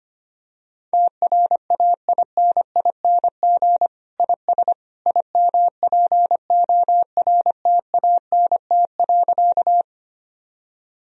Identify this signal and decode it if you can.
Morse code, heard as a keyed tone, about 25 wpm, 700 Hz. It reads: TRAINING IS IMPORTANT.
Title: Sentences from Top 1000 Words - Review Part 4 - 25wpm